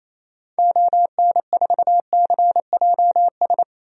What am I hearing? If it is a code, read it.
Morse code, heard as a keyed tone, about 28 wpm, 700 Hz. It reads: ON4CJS
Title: DXCC-100 Call Signs - 6 Characters - 28wpm